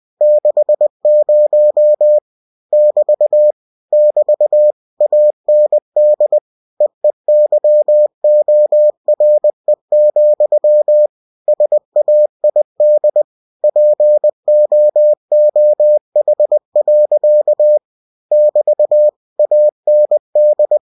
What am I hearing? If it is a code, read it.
Morse code, heard as a keyed tone, about 20 wpm, 600 Hz. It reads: 60 <BT> <BT> AND EEYORE, SAID POOH. <BT> AND